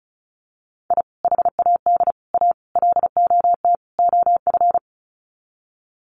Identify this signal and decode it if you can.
Morse code, heard as a keyed tone, about 34 wpm, 700 Hz. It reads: I HAD A LOT OF
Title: Sets of 5 Words Encore 35wpm